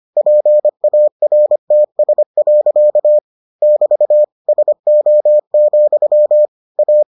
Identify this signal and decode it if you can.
Morse code, heard as a keyed tone, about 25 wpm, 600 Hz. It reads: PARTS. <BT> SO, A